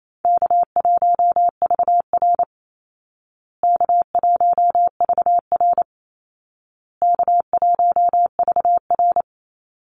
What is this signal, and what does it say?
Morse code, heard as a keyed tone, about 28 wpm, 700 Hz. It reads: K1VR K1VR K1VR
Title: Easy Call Signs Repeated 3x - 28wpm